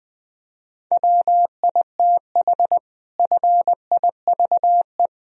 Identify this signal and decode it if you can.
Morse code, heard as a keyed tone, about 20 wpm, 700 Hz. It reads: WITH FIVE
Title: Sentences from Top 1000 Words - Review Part 3 - 20wpm